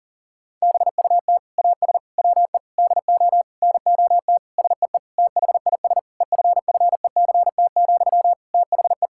Morse code, heard as a keyed tone, about 37 wpm, 700 Hz. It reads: BUT AS WE DO NOT SEE THIS EFFECT, THE
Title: Sentences from Top 1600 Words 40wpm